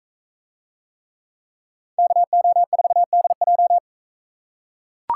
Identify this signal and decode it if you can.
Morse code, heard as a keyed tone, about 41 wpm, 700 Hz. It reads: KO4DJ